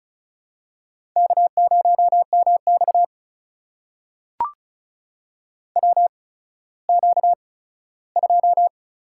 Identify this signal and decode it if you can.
Morse code, heard as a keyed tone, about 36 wpm, spaced as for 17 wpm, 700 Hz. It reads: K0MX WQ2
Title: USA Call Signs - 4 Characters - 8x Character Spacing - 35wpm